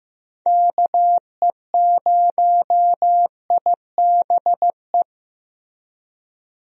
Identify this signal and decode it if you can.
Morse code, heard as a keyed tone, about 15 wpm, 700 Hz. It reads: KE0IBE
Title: USA Call Signs - 6 Characters - Repeated 3x - 15wpm